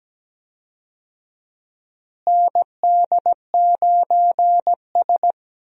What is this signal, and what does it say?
Morse code, heard as a keyed tone, about 17 wpm, 700 Hz. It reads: ND9S
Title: USA Call Signs - 4 Characters - Repeated 3x - 17wpm